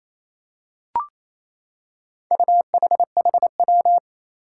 Morse code, heard as a keyed tone, about 28 wpm, 700 Hz. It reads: UHHW
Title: Four Challenging Letter-Number Combinations 28wpm